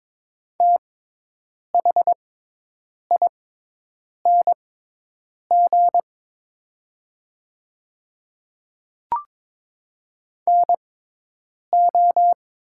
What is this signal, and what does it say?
Morse code, heard as a keyed tone, about 22 wpm, spaced as for 8 wpm, 700 Hz. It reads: THING NO
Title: 6x Character Spacing - Top 100 Words - 22wpm